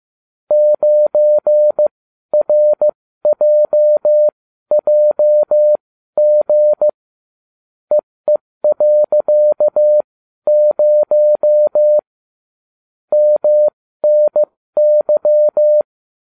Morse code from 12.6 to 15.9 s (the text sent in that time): MNY